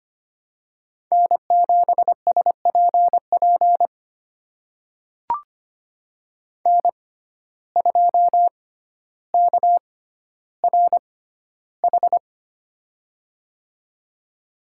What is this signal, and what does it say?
Morse code, heard as a keyed tone, about 25 wpm, spaced as for 13 wpm, 700 Hz. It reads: N7SPP N2KRH
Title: USA Call Signs - 6x Character Spacing - 25wpm